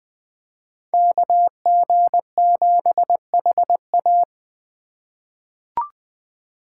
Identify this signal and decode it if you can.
Morse code, heard as a keyed tone, about 20 wpm, 700 Hz. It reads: KG7HA
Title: USA Call Signs - 5 Characters - 3x Character Spacing - 20wpm